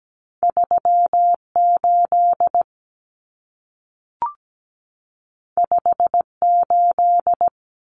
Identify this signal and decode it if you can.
Morse code, heard as a keyed tone, about 17 wpm, 700 Hz. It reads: 38 58
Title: Common Cut Numbers - 2-Digit 17wpm